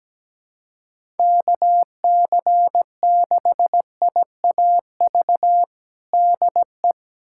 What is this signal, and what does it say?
Morse code, heard as a keyed tone, about 17 wpm, 700 Hz. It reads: KC6IAV DE
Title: USA Callsign DE Callsign 17wpm